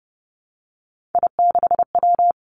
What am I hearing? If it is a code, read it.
Morse code, heard as a keyed tone, about 32 wpm, 700 Hz. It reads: I6W
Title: Three Challenging Letter-Number Combinations 30wpm